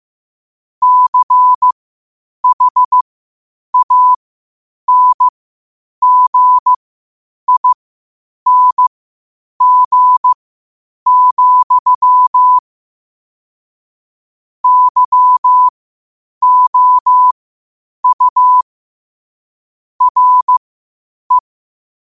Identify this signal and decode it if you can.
Morse code, heard as a keyed tone, about 15 wpm, spaced as for 8 wpm, 1000 Hz. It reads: CHANGING, YOU RE
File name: qotdfiles / 8wpm